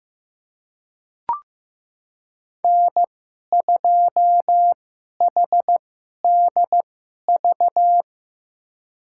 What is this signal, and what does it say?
Morse code, heard as a keyed tone, about 15 wpm, spaced as for 11 wpm, 700 Hz. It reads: N2HDV
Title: USA Call Signs - 5 Characters - 2x Character Spacing - 15wpm